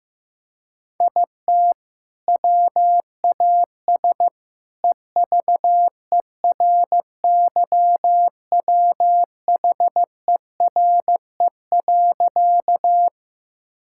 Morse code, heard as a keyed tone, about 15 wpm, 700 Hz. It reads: IT WAS EVERYWHERE.
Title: Sentences from Top 1600 Words 15wpm